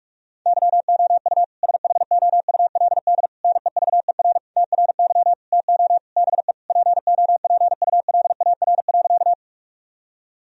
Morse code, heard as a keyed tone, about 46 wpm, 700 Hz. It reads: YOU SHOULD NEVER TRY TO BE POPULAR.
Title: Sentences from Top 1000 Words - Review Part 3 - 45wpm